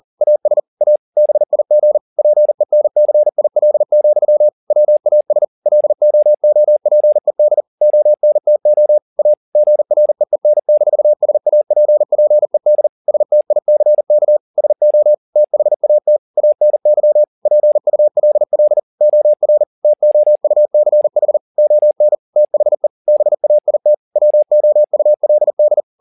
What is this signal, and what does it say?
Morse code, heard as a keyed tone, about 40 wpm, 600 Hz. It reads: AS A BIG PENCIL, WAS LOOPED ONTO A GREEN-SAPPED STICK SO THAT ANY PULL OR TOUCH ON THE BAIT WOULD